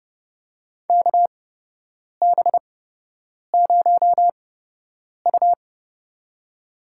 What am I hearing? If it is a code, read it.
Morse code, heard as a keyed tone, about 30 wpm, spaced as for 8 wpm, 700 Hz. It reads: KB0U
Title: USA Call Signs - 4 Characters - 8x Character Spacing - 30wpm